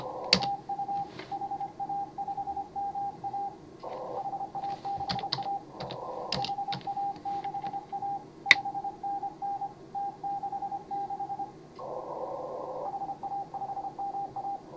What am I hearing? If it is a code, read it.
Morse code, heard as a keyed tone, about 26 wpm, 810 Hz, with noise. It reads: TU HA5MA HA6A HA6A 5NN T66 HA5MA